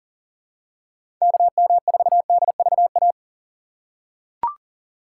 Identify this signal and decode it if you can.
Morse code, heard as a keyed tone, about 40 wpm, 700 Hz. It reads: KM4DVA
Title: USA Call Signs - 6 Characters - 3x Character Spacing - 40wpm